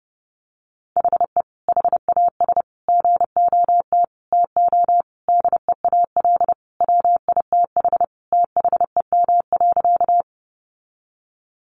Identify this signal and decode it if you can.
Morse code, heard as a keyed tone, about 30 wpm, 700 Hz. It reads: HE HAS GOT TO DEAL WITH THEM.
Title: Sentences from Top 800 Words 30wpm